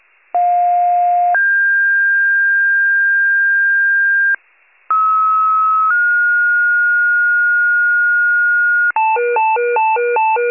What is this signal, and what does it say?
Two-tone page, heard as a one-second tone, then a three-second tone, about 710 then 1600 Hz, about 1290 then 1430 Hz, with noise. It is heard from 0.3 to 8.9 s.
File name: pager / 84 FIRE / wcalert